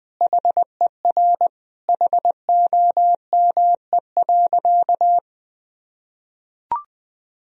Morse code, heard as a keyed tone, about 20 wpm, 700 Hz. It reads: HER HOME.
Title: Sentences from Top 1000 Words - Review Part 3 - 20wpm